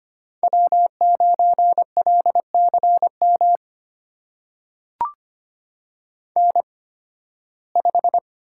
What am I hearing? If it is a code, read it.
Morse code, heard as a keyed tone, about 25 wpm, spaced as for 16 wpm, 700 Hz. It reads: W9LCM N5
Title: USA Call Signs - 5 Characters - 8x Character Spacing - 25wpm